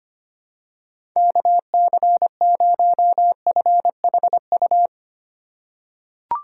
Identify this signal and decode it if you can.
Morse code, heard as a keyed tone, about 25 wpm, 700 Hz. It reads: KC0FHU
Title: USA Call Signs - 6 Characters - 4x Character Spacing - 25wpm